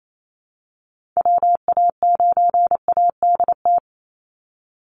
Morse code, heard as a keyed tone, about 28 wpm, 700 Hz. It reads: WA9ADT
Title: Call Signs Repeated 3x - 28wpm